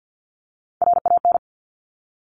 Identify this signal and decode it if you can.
Morse code, heard as a keyed tone, about 50 wpm, 700 Hz. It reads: RAN
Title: Top Three Letter Combinations in Words 50wpm